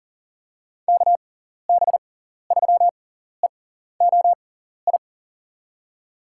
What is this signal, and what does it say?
Morse code, heard as a keyed tone, about 40 wpm, spaced as for 14 wpm, 700 Hz. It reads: KB3EOI
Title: USA Call Signs - 6 Characters - 6x Character Spacing - 40wpm